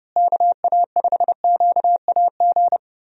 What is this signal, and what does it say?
Morse code, heard as a keyed tone, about 30 wpm, 700 Hz. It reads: KA5QAG
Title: USA Call Signs - 6 Characters - Repeated 3x - 30wpm